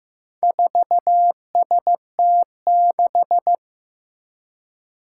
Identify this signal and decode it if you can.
Morse code, heard as a keyed tone, about 15 wpm, 700 Hz. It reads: 4ST6
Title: Four Challenging Letter-Number Combinations 15wpm